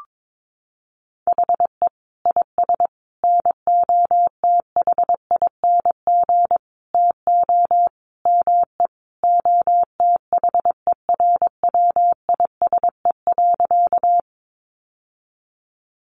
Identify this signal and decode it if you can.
Morse code, heard as a keyed tone, about 22 wpm, 700 Hz. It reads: HE IS NOTHING TO ME OTHERWISE.
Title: Sentences from Top 1200 Words 22wpm